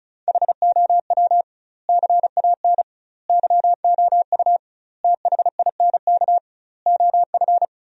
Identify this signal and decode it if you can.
Morse code, heard as a keyed tone, about 36 wpm, 700 Hz. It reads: HOW CAN YOU THINK OF
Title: 2x Word-Spacing - Sentences from Top 200 Words - 35wpm